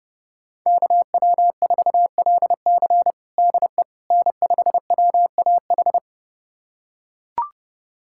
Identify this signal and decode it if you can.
Morse code, heard as a keyed tone, about 32 wpm, 700 Hz. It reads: KW4LC DE N5WAH
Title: USA Callsign DE Callsign - 5 Characters - 30wpm